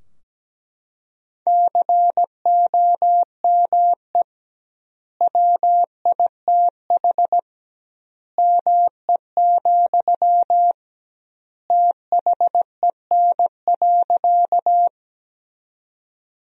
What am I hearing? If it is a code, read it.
Morse code, heard as a keyed tone, about 17 wpm, 700 Hz. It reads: COME WITH ME, THEN.
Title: 2x Word-Spacing - Sentences from Top 200 Words - 17wpm